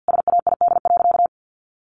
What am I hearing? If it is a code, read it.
Morse code, heard as a keyed tone, about 48 wpm, 700 Hz. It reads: SAID.
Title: Sentences from Top 1100 Words 50wpm